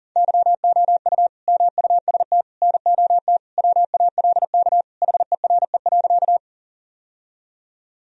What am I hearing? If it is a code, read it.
Morse code, heard as a keyed tone, about 40 wpm, 700 Hz. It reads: YOU MUST NOT WALK HERE.
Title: Sentences from Top 1000 Words - Review Part 1 - 40wpm